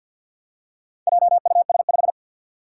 Morse code, heard as a keyed tone, about 51 wpm, 700 Hz. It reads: JUS5
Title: Four Challenging Letter-Number Combinations 50wpm